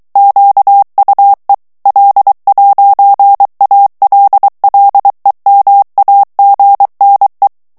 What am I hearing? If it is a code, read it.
Morse code, heard as a keyed tone, about 23 wpm, 800 Hz. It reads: QUE L'ALLEMAGNE
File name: morse mp3 / FRANCAIS TEXTE CLAIR 1 VITESSE 22 MOTS MINUTES